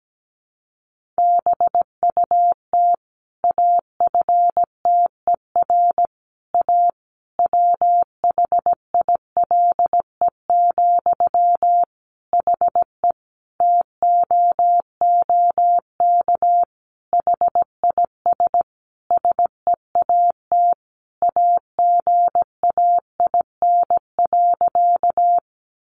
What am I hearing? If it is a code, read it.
Morse code, heard as a keyed tone, about 17 wpm, 700 Hz. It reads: BUT AFTER A WHILE, HE TOOK HIS SEAT AGAIN.
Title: Sentences from Top 1500 Words 17wpm